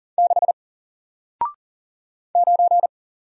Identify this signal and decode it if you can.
Morse code, heard as a keyed tone, about 40 wpm, 700 Hz. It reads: 6 9
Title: Single Character (Letter:Number:Punctuation:Prosign) Alt 40wpm